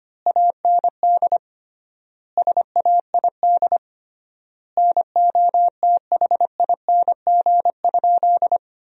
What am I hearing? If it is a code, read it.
Morse code, heard as a keyed tone, about 25 wpm, 700 Hz. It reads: AND SAID NOTHING?
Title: 3x Word-Spacing - Sentences from Top 500 Words - 25wpm